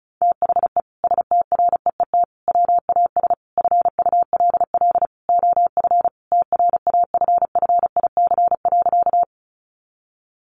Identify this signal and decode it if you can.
Morse code, heard as a keyed tone, about 34 wpm, 700 Hz. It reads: THE STREET WAS FULL OF TRAFFIC.
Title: Sentences from Top 1300 Words 35wpm